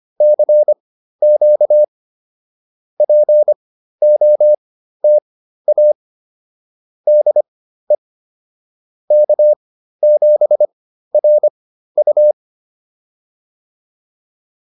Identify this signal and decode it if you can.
Morse code, heard as a keyed tone, about 25 wpm, spaced as for 13 wpm, 600 Hz. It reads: CQ POTA DE K7RU